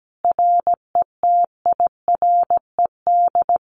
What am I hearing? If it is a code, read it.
Morse code, heard as a keyed tone, about 17 wpm, 700 Hz. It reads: RETIRED
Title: Common QSO Phrases Encore 17wpm